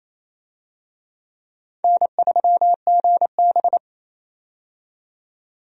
Morse code, heard as a keyed tone, about 29 wpm, 700 Hz. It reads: N3GB
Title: USA Call Signs - 4 Characters - 28wpm